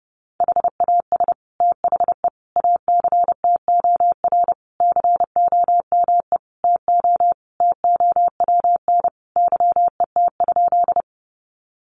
Morse code, heard as a keyed tone, about 30 wpm, 700 Hz. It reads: HAS THE ACTOR COME TO TOWN YET?
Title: Sentences from Top 1500 Words 30wpm